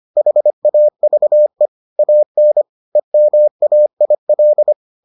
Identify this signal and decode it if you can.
Morse code, heard as a keyed tone, about 25 wpm, 600 Hz. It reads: HAVE AN EMAIL